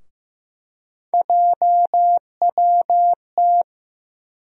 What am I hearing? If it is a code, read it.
Morse code, heard as a keyed tone, about 15 wpm, 700 Hz. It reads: JWT
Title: USA Call Sign Postfixes - 3 Letters 15wpm